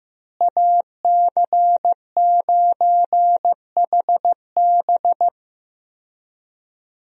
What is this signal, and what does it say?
Morse code, heard as a keyed tone, about 15 wpm, 700 Hz. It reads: AC9HB
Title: USA Call Signs - 5 Characters - Repeated 3x - 15wpm